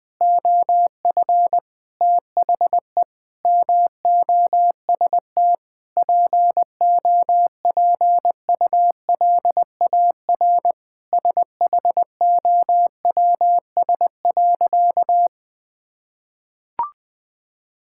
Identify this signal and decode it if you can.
Morse code, heard as a keyed tone, about 21 wpm, 700 Hz. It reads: OF THE MOST POPULAR SHOWS.
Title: Sentences with 5-Word Phrases - 20wpm